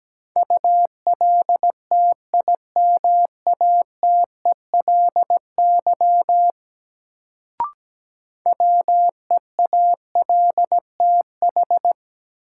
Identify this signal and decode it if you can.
Morse code, heard as a keyed tone, about 17 wpm, 700 Hz. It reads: ULTIMATELY WEALTH
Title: Sentences from Top 1200 Words - New Word Drill Alt - 17wpm